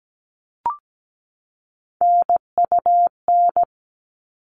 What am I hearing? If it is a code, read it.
Morse code, heard as a keyed tone, about 17 wpm, 700 Hz. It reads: NUN
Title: All Cut Numbers - 3-Digit 17wpm